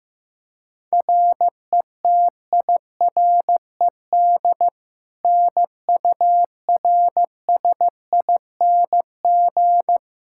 Morse code, heard as a keyed tone, about 15 wpm, 700 Hz. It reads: RETIRED NURSING
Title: Practice QSO - Exchange 3 - More Info - 15wpm